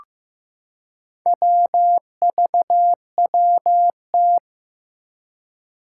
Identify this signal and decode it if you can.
Morse code, heard as a keyed tone, about 15 wpm, 700 Hz. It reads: WVWT
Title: Four Challenging Letter-Number Combinations 15wpm